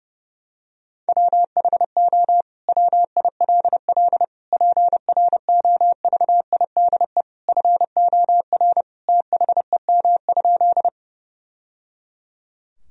Morse code, heard as a keyed tone, about 30 wpm, 700 Hz. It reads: WHO WILL PROVIDE FOR THEM?
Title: Sentences from Top 1000 Words - Review Part 3 - 30wpm